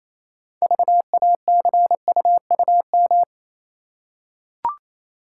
Morse code, heard as a keyed tone, about 28 wpm, 700 Hz. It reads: VACUUM